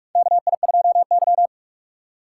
Morse code, heard as a keyed tone, about 42 wpm, 700 Hz. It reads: KI2Y
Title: USA Call Signs - 4 Characters - 4x Character Spacing - 45wpm